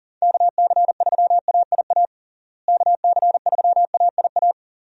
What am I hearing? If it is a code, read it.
Morse code, heard as a keyed tone, about 40 wpm, 700 Hz. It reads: KC3AIA KC3AIA
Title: USA Call Signs - 6 Characters - Repeated 3x - 40wpm